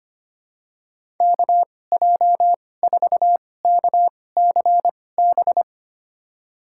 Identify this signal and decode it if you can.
Morse code, heard as a keyed tone, about 25 wpm, spaced as for 18 wpm, 700 Hz. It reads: KJ4KCB
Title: USA Call Signs - 6 Characters - 2x Character Spacing - 25wpm